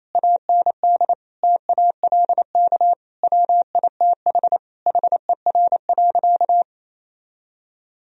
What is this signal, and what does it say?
Morse code, heard as a keyed tone, about 28 wpm, 700 Hz. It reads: AND TALK WITH HER.
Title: Sentences from Top 1000 Words - Review Part 4 - 28wpm